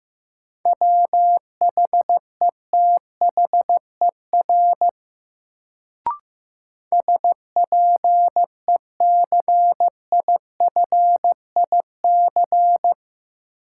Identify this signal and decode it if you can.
Morse code, heard as a keyed tone, about 15 wpm, 700 Hz. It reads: WHETHER SPECIFIC
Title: Sentences from Top 1000 Words - Review Part 3 - New Word Drill - 15wpm